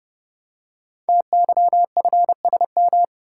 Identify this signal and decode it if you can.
Morse code, heard as a keyed tone, about 30 wpm, 700 Hz. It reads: TYFSM
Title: Code Groups - 5 Letters - 30wpm